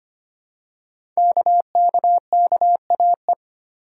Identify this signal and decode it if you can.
Morse code, heard as a keyed tone, about 25 wpm, 700 Hz. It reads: KKKAE
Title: Code Groups - 5 Letters - 25wpm